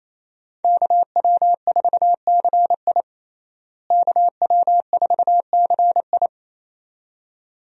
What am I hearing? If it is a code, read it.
Morse code, heard as a keyed tone, about 27 wpm, 700 Hz. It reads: KW4CI KW4CI
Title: USA Call Signs - 5 Characters - Repeated 2x - 28wpm